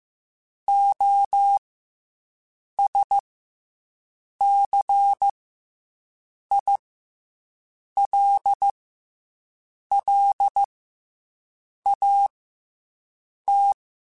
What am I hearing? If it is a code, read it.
Morse code, heard as a keyed tone, about 15 wpm, 800 Hz. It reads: O S C I L L A T
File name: Special INGs / Special INGs-mp3s / SlowToFaster ing @13